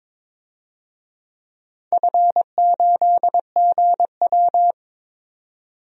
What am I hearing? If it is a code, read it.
Morse code, heard as a keyed tone, about 22 wpm, 700 Hz. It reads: F8GW
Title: DXCC-100 Call Signs - 4 Characters - 22wpm